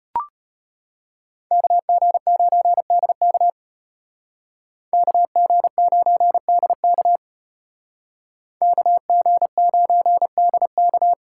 Morse code, heard as a keyed tone, about 35 wpm, 700 Hz. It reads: KG9DK KG9DK KG9DK